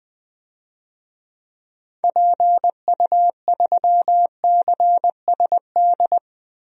Morse code, heard as a keyed tone, about 20 wpm, 700 Hz. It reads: PU3CSD
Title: DXCC-100 Call Signs - 6 Characters - Repeated 2x - 20wpm